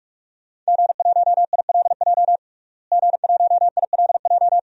Morse code, heard as a keyed tone, about 41 wpm, 700 Hz. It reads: G1ILJ G1ILJ
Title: DXCC-100 Call Signs - 5 Characters - Repeated 2x - 45wpm